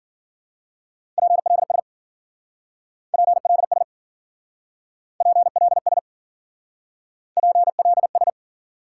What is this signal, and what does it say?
Morse code, heard as a keyed tone, about 50 wpm, 700 Hz. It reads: PLS PLS PLS PLS